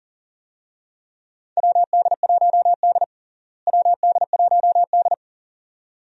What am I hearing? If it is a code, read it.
Morse code, heard as a keyed tone, about 40 wpm, 700 Hz. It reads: WD1D WD1D
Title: USA Call Signs - 4 Characters - Repeated 2x - 40wpm